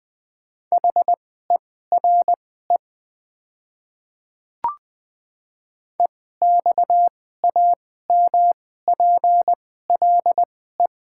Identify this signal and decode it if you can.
Morse code, heard as a keyed tone, about 20 wpm, spaced as for 14 wpm, 700 Hz. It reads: HERE EXAMPLE
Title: 2x Character Spacing - Top 500 Words - 20wpm